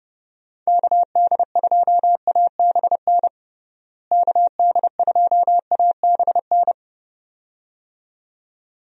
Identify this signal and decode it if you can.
Morse code, heard as a keyed tone, about 32 wpm, 700 Hz. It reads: KD2ABN KD2ABN
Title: USA Call Signs - 6 Characters - Repeated 2x - 30wpm